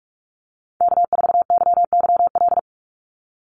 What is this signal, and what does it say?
Morse code, heard as a keyed tone, about 44 wpm, 700 Hz. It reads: K4YYL
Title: Call Signs Repeated 2x - 45wpm